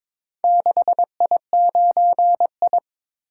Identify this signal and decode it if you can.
Morse code, heard as a keyed tone, about 23 wpm, 700 Hz. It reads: 6I9I
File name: practice / 22wpm / Four Letter-Number Combinations - 22wpm